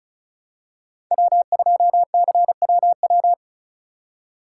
Morse code, heard as a keyed tone, about 36 wpm, 700 Hz. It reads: W2CWW
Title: USA Call Signs - 5 Characters - 4x Character Spacing - 35wpm